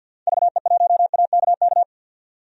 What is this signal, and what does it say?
Morse code, heard as a keyed tone, about 47 wpm, 700 Hz. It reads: VE1AKK
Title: DXCC-100 Call Signs - 6 Characters - 50wpm